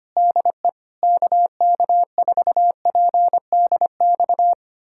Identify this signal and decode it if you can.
Morse code, heard as a keyed tone, about 25 wpm, 700 Hz. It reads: DE KK4PDX
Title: USA Callsign DE Callsign - 6 Characters - 25wpm